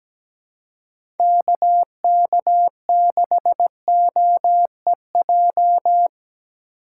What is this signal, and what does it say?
Morse code, heard as a keyed tone, about 17 wpm, 700 Hz. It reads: KK6OEJ
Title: USA Call Signs - 4x Character Spacing - 17wpm